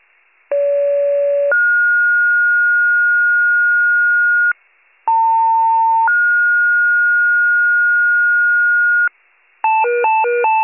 Two-tone page, heard as a one-second tone, then a three-second tone, about 570 then 1430 Hz, about 900 then 1430 Hz. It is heard from 0.5 to 9.1 s.